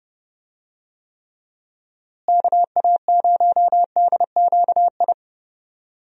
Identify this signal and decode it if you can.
Morse code, heard as a keyed tone, about 30 wpm, 700 Hz. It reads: KA0DQI
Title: USA Call Signs - 6 Characters - Repeated 2x - 30wpm